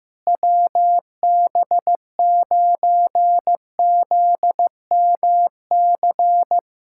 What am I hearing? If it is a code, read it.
Morse code, heard as a keyed tone, about 14 wpm, 700 Hz. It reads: WB9ZMC
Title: USA Callsign DE Callsign 15wpm